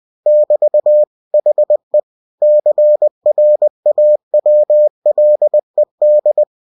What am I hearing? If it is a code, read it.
Morse code, heard as a keyed tone, about 20 wpm, 600 Hz. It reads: <BT> HE CRAWLED
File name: cw-books / winnie-the-pooh / winnie-20wpm-page-0004